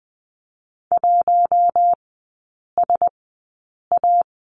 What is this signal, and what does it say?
Morse code, heard as a keyed tone, about 20 wpm, spaced as for 8 wpm, 700 Hz. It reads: 1SA
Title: 087 Lesson 15e - Number 5 - Three Characters 20wpm